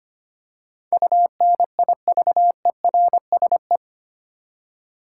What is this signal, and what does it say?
Morse code, heard as a keyed tone, about 25 wpm, 700 Hz. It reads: UNIVERSE
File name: Eight Letter Words - 25wpm